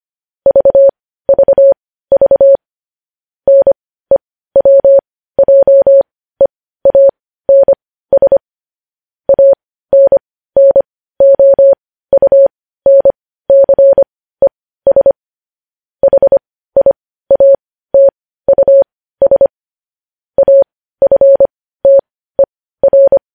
Morse code, heard as a keyed tone, about 25 wpm, spaced as for 15 wpm, 550 Hz. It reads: VVV NEWJEANS ANNOUNCES HIATUS AFTER
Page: news-headlines-in-morse-code-at-15-wpm